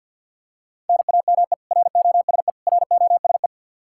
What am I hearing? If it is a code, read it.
Morse code, heard as a keyed tone, about 52 wpm, 700 Hz. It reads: NAME ROSE ROSE
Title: Practice QSO - Exchange 1 - Beginning - 50wpm